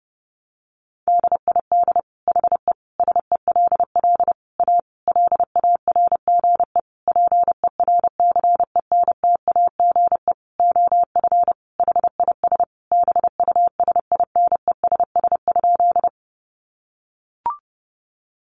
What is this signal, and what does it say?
Morse code, heard as a keyed tone, about 30 wpm, 700 Hz. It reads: DID HE SELL A LARGE PERCENTAGE OF HIS BUSINESS?
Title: Sentences from Top 1300 Words 30wpm